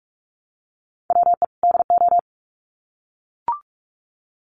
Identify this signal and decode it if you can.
Morse code, heard as a keyed tone, about 44 wpm, 700 Hz. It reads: WE DO
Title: Sets of 2 Words 45wpm